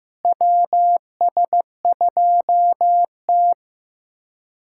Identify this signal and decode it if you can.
Morse code, heard as a keyed tone, about 16 wpm, 700 Hz. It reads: WS2T
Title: USA Callsign DE Callsign - 4 Characters - 15wpm